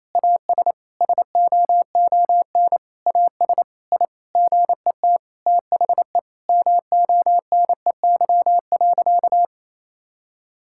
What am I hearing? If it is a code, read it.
Morse code, heard as a keyed tone, about 27 wpm, 700 Hz. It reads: AS SOON AS I GET THE MONEY.
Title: Sentences from Top 1000 Words - Review Part 5 - 28wpm